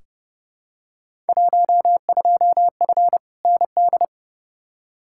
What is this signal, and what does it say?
Morse code, heard as a keyed tone, about 31 wpm, 700 Hz. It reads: 12F ND
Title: Field Day Exchanges Encore - 30wpm